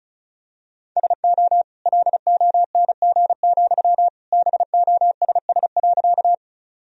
Morse code, heard as a keyed tone, about 36 wpm, 700 Hz. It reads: SO LONG, BOSS.
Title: Sentences from Top 1000 Words - Review Part 4 - 35wpm